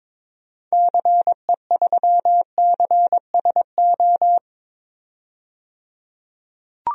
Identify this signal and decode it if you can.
Morse code, heard as a keyed tone, about 22 wpm, 700 Hz. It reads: CE3CSO
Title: DXCC-100 Call Signs - 6 Characters - 22wpm